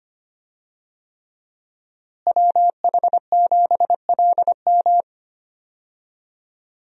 Morse code, heard as a keyed tone, about 25 wpm, 700 Hz. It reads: WH7LM
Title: USA Call Signs - 5 Characters - Repeated 3x - 25wpm